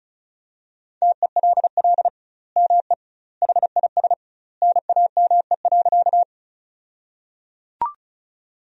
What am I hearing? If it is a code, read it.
Morse code, heard as a keyed tone, about 36 wpm, 700 Hz. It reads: TELL ME HIS NAME.
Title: 2x Word-Spacing - Sentences from Top 200 Words - 35wpm